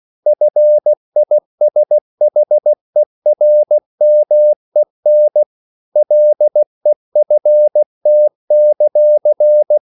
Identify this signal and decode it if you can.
Morse code, heard as a keyed tone, about 16 wpm, 600 Hz. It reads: FISHERMEN LEFT;